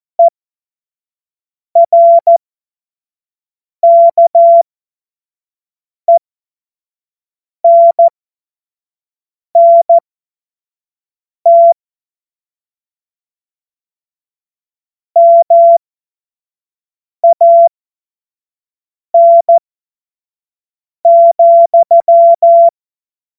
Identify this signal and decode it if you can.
Morse code, heard as a keyed tone, about 14 wpm, spaced as for 5 wpm, 680 Hz. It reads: ERKENNT MAN,